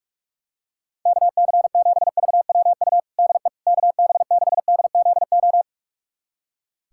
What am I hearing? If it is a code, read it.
Morse code, heard as a keyed tone, about 44 wpm, 700 Hz. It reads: KC7VWU DE KB6DZO
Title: USA Callsign DE Callsign 45wpm